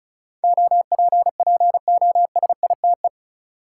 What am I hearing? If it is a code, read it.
Morse code, heard as a keyed tone, about 34 wpm, 700 Hz. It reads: OPPOSITE
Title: Sentences from Top 1300 Words - New Word Drill Alt - 35wpm